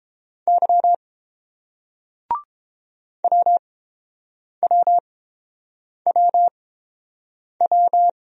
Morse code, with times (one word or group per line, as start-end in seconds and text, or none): 0.5-0.9 Y
3.2-8.1 WWWW